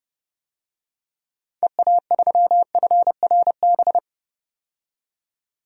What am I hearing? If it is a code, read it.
Morse code, heard as a keyed tone, about 30 wpm, 700 Hz. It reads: EA3FRB
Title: DXCC-100 Call Signs - 6 Characters - 30wpm